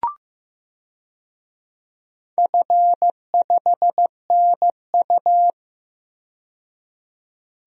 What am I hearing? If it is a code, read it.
Morse code, heard as a keyed tone, about 15 wpm, 700 Hz. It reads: F5NU
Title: DXCC-100 Call Signs - 4 Characters - 15wpm